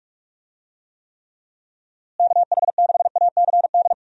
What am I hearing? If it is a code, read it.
Morse code, heard as a keyed tone, about 41 wpm, 700 Hz. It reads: KH6ACD
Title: USA Call Signs - 6 Characters - Repeated 3x - 45wpm